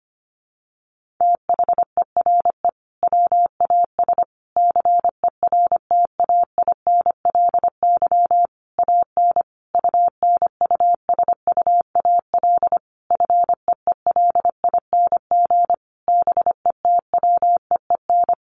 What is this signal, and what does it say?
Morse code, heard as a keyed tone, about 25 wpm, 700 Hz. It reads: THERE WAS CERTAINLY AN UNUSUAL FEELING BETWEEN
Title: Sentences from Top 1400 Words 25wpm